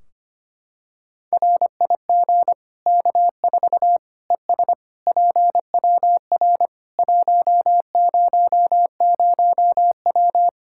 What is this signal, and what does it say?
Morse code, heard as a keyed tone, about 25 wpm, 700 Hz. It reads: RIG K4 ES PWR 100W
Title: Practice QSO - Exchange 2 - Info - 25wpm